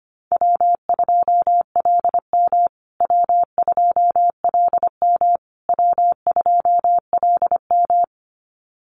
Morse code, heard as a keyed tone, about 25 wpm, 700 Hz. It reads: W2LM W2LM W2LM
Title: USA Call Signs Repeated 3x 25wpm